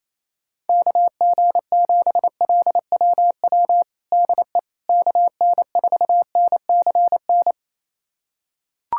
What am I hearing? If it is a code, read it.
Morse code, heard as a keyed tone, about 28 wpm, 700 Hz. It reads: KG7LWW DE KN4NCN
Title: USA Callsign DE Callsign 28wpm